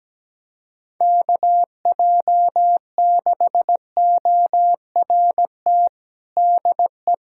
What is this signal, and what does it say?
Morse code, heard as a keyed tone, about 17 wpm, 700 Hz. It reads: KJ6ORT DE
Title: USA Callsign DE Callsign - 6 Characters - 17wpm